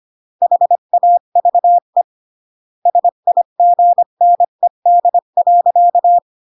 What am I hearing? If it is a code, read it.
Morse code, heard as a keyed tone, about 25 wpm, 700 Hz. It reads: HAVE SIGNED.